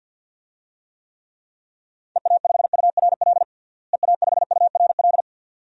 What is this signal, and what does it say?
Morse code, heard as a keyed tone, about 51 wpm, 700 Hz. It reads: EA5URL EA5URL
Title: DXCC-100 Call Signs - 6 Characters - Repeated 2x - 50wpm